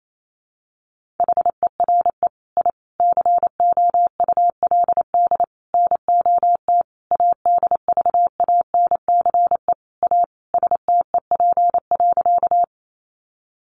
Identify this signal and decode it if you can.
Morse code, heard as a keyed tone, about 28 wpm, 700 Hz. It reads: HERE I COULD NOT ADVANCE A STEP.
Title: Sentences from Top 1400 Words 28wpm